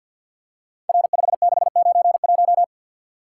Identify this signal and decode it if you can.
Morse code, heard as a keyed tone, about 49 wpm, 700 Hz. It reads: R5691
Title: Code Groups - 5 Letter:Numbers - 50wpm